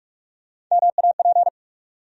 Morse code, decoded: MAP

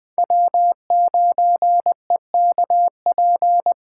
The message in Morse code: W9EKP

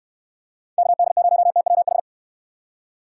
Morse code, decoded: KD0TWH

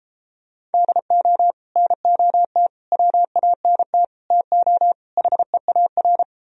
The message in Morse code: DO NOT WANT TO HEAR

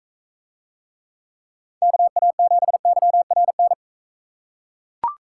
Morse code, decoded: KA7YRN

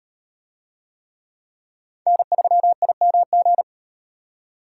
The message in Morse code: N3IMG

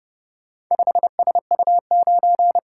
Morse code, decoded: 5SU9

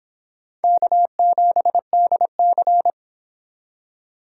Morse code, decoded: K7DC

